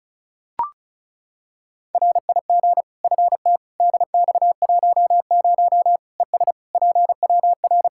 Morse code, heard as a keyed tone, about 36 wpm, 700 Hz. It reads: RIG FT DX10 ES PWR
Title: Practice QSO - Exchange 2 - Info - 35wpm